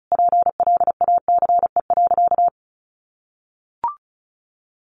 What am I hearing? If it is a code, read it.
Morse code, heard as a keyed tone, about 34 wpm, 700 Hz. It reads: PLACE.
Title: Sentences from Top 1500 Words 35wpm